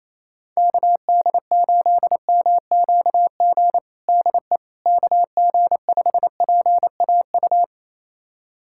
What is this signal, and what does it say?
Morse code, heard as a keyed tone, about 27 wpm, 700 Hz. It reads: KD8MQG DE KG5PAU
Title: USA Callsign DE Callsign 28wpm